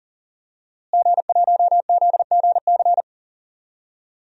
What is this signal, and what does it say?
Morse code, heard as a keyed tone, about 38 wpm, 700 Hz. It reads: G1ZGC
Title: DXCC-100 Call Signs - 5 Characters - Repeated 3x - 40wpm